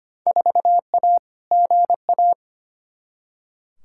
Morse code, heard as a keyed tone, about 25 wpm, 700 Hz. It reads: 4A GA
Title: Field Day Exchanges - 25wpm